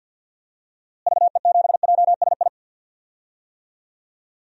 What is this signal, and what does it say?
Morse code, heard as a keyed tone, about 46 wpm, 700 Hz. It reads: VE7JSI
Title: DXCC-100 Call Signs - 6 Characters - Repeated 3x - 50wpm